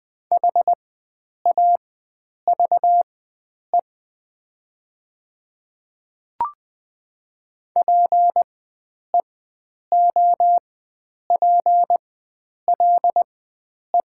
Morse code, heard as a keyed tone, about 20 wpm, spaced as for 9 wpm, 690 Hz. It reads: HAVE PEOPLE
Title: 4x Character Spacing - Top 100 Words - 20wpm